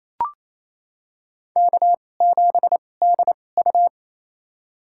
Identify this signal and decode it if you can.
Morse code, heard as a keyed tone, about 28 wpm, spaced as for 20 wpm, 700 Hz. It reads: K7DU
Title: USA Call Signs - 4 Characters - 2x Character Spacing - 28wpm